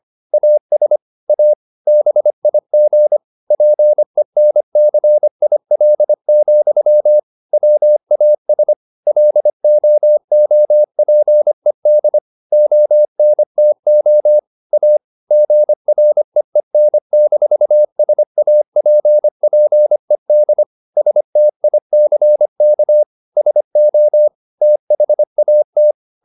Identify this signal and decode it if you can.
Morse code, heard as a keyed tone, about 25 wpm, 600 Hz. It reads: AS A BIG PENCIL, WAS LOOPED ONTO A GREEN-SAPPED STICK SO THAT
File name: the old man and the sea part 1 25wpm